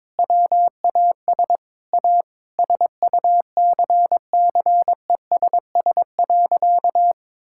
Morse code, heard as a keyed tone, about 22 wpm, 700 Hz. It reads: WAS A SUCCESS.
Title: Sentences from Top 1000 Words - Review Part 5 - 22wpm